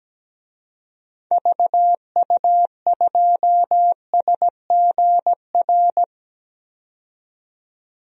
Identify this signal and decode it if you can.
Morse code, heard as a keyed tone, about 17 wpm, 700 Hz. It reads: VU2SGR
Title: DXCC-100 Call Signs - 6 Characters - 17wpm